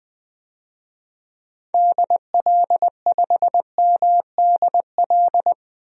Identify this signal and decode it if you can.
Morse code, heard as a keyed tone, about 20 wpm, 700 Hz. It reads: DL5MDL
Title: DXCC-100 Call Signs - 6 Characters - Repeated 2x - 20wpm